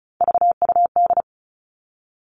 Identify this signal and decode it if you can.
Morse code, heard as a keyed tone, about 34 wpm, 700 Hz. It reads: VUD
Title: All Cut Numbers - 3-Digit 35wpm